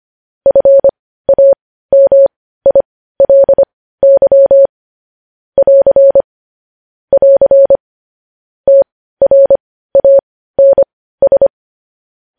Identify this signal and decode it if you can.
Morse code, heard as a keyed tone, about 25 wpm, spaced as for 15 wpm, 550 Hz. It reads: FAMILY <AR> <AR> TRANS
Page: News headlines in morse code at 15 wpm